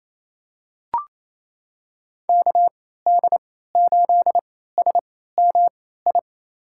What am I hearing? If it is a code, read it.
Morse code, heard as a keyed tone, about 28 wpm, spaced as for 16 wpm, 700 Hz. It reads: KD8SMI
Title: USA Call Signs - 6 Characters - 3x Character Spacing - 28wpm